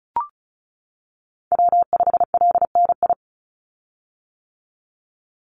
Morse code, heard as a keyed tone, about 34 wpm, 700 Hz. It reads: W5LNI
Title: Call Signs - 35wpm